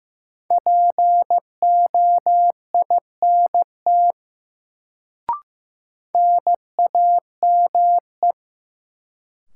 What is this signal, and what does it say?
Morse code, heard as a keyed tone, about 15 wpm, 700 Hz. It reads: POINT NAME
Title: Sentences from Top 200 Words - New Word Drill - 15wpm